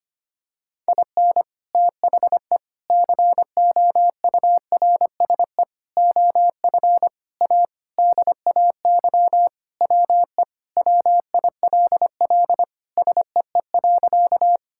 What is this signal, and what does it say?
Morse code, heard as a keyed tone, about 25 wpm, 700 Hz. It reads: IN THE COURSE OF A DAY WE WILL SEE.
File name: practice / 25wpm / Sentences with 3-Word Phrases - 25wpm